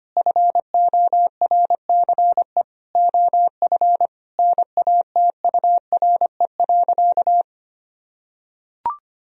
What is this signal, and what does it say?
Morse code, heard as a keyed tone, about 25 wpm, 700 Hz. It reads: FORCE OF NATURE.
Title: Sentences from Top 1000 Words - Review Part 4 - 25wpm